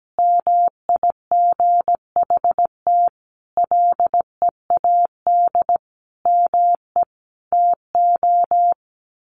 Morse code, heard as a keyed tone, about 17 wpm, 700 Hz. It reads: MIGHT LEAD ME TO
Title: Sentences from Top 1200 Words 17wpm